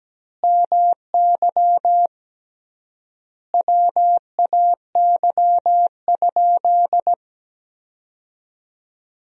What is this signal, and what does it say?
Morse code, heard as a keyed tone, about 17 wpm, 700 Hz. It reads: MY WAY?
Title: 3x Word-Spacing - Sentences from Top 200 Words - 17wpm